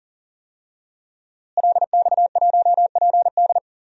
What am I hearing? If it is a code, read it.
Morse code, heard as a keyed tone, about 40 wpm, 700 Hz. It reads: LX1PD
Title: DXCC-100 Call Signs - 5 Characters - Repeated 3x - 40wpm